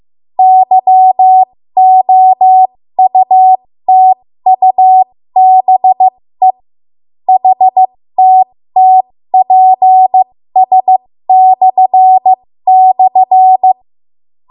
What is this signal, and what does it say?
Morse code, heard as a keyed tone, about 15 wpm, spaced as for 13 wpm, 750 Hz. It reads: YOUTUBE HTTPS//
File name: Morse / 13WPM